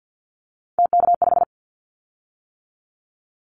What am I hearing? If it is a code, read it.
Morse code, heard as a keyed tone, about 50 wpm, 700 Hz. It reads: TK5
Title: Callsign Prefixes for Top 100 DXCC - 50wpm